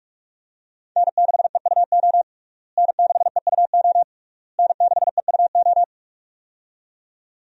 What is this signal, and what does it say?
Morse code, heard as a keyed tone, about 46 wpm, 700 Hz. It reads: N6EUO N6EUO N6EUO
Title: USA Call Signs - 5 Characters - Repeated 3x - 45wpm